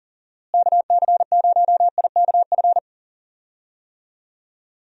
Morse code, heard as a keyed tone, about 40 wpm, 700 Hz. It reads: KC0IKF
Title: USA Call Signs - 6 Characters - Repeated 3x - 40wpm